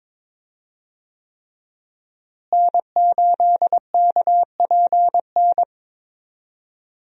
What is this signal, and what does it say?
Morse code, heard as a keyed tone, about 22 wpm, 700 Hz. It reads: N8KPN
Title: USA Call Signs - 5 Characters - Repeated 3x - 22wpm